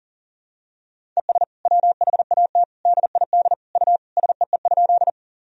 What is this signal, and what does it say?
Morse code, heard as a keyed tone, about 40 wpm, 700 Hz. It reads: ES WHAT DID U SEE?
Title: Sentences from Top 100 Words and Abbreviations 40wpm